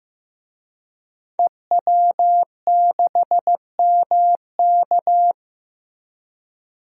Morse code, heard as a keyed tone, about 15 wpm, 700 Hz. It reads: EW6MK
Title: DXCC-100 Call Signs - 5 Characters - Repeated 3x - 15wpm